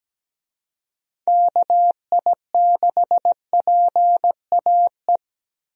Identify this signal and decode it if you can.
Morse code, heard as a keyed tone, about 17 wpm, 700 Hz. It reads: KI6PAE
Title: USA Call Signs - 6 Characters - 17wpm